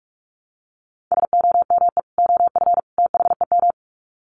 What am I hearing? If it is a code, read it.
Morse code, heard as a keyed tone, about 44 wpm, 700 Hz. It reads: SOME OF THEM
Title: Sets of 3 Words 45wpm